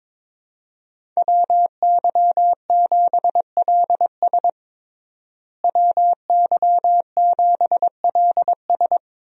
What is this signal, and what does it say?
Morse code, heard as a keyed tone, about 22 wpm, 700 Hz. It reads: WY7LS WY7LS
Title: USA Call Signs - 5 Characters - Repeated 2x - 22wpm